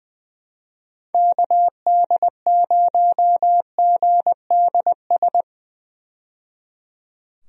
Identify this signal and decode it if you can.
Morse code, heard as a keyed tone, about 20 wpm, 700 Hz. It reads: KD0GDS
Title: USA Call Signs - 6 Characters - 20wpm